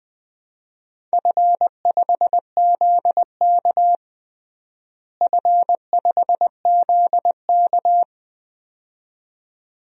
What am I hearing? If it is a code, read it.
Morse code, heard as a keyed tone, about 20 wpm, 700 Hz. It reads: F5ZK F5ZK
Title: DXCC-100 Call Signs - 4 Characters - Repeated 2x - 20wpm